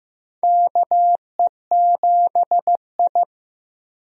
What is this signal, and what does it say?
Morse code, heard as a keyed tone, about 15 wpm, 700 Hz. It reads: KE7I
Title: USA Call Signs - 4 Characters - Repeated 2x - 15wpm